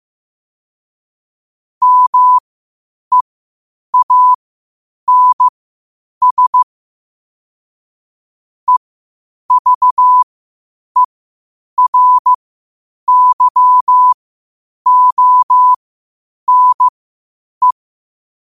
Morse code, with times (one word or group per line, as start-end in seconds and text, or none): 1.8-6.6 MEANS
8.7-17.7 EVERYONE